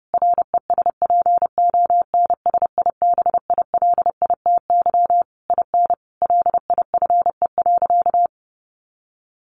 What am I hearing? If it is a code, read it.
Morse code, heard as a keyed tone, about 30 wpm, 700 Hz. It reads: RESPONSIBILITY IN LIFE.
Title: Sentences from Top 1200 Words 30wpm